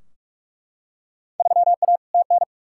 Morse code, heard as a keyed tone, about 41 wpm, 700 Hz. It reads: 3A TN
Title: Field Day Exchanges - 45wpm